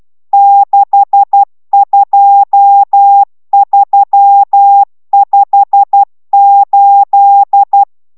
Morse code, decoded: 62358

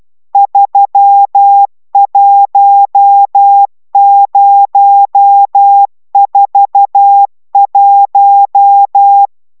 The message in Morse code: 31041

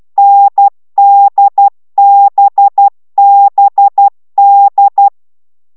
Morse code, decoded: NDBBD

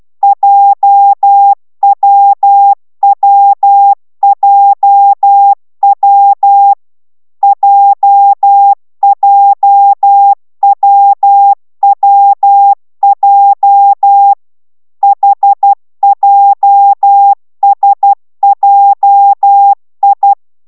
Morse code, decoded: JWWJW JJWWJ HJSJI

